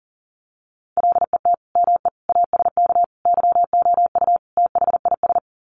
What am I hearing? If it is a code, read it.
Morse code, heard as a keyed tone, about 40 wpm, 700 Hz. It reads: LET ME ASK YOU THIS
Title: Sets of 5 Words 40wpm